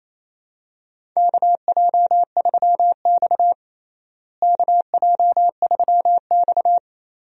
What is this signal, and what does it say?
Morse code, heard as a keyed tone, about 28 wpm, 700 Hz. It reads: KJ3X KJ3X
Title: USA Call Signs - 4 Characters - Repeated 2x - 28wpm